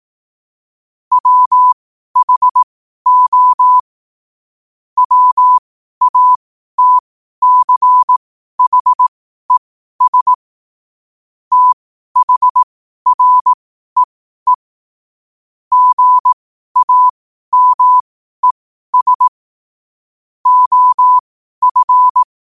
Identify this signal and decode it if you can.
Morse code, heard as a keyed tone, about 18 wpm, spaced as for 12 wpm, 1000 Hz. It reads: WHO WATCHES THREE GAMES OF
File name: qotdfiles / 12wpm